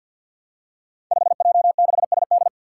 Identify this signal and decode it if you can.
Morse code, heard as a keyed tone, about 50 wpm, 700 Hz. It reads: 5J6SD
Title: Five Challenging Letter-Number Combinations 50wpm